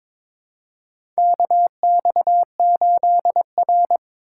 Morse code, heard as a keyed tone, about 22 wpm, 700 Hz. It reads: KX8R
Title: USA Call Signs - 4 Characters - 2x Character Spacing - 22wpm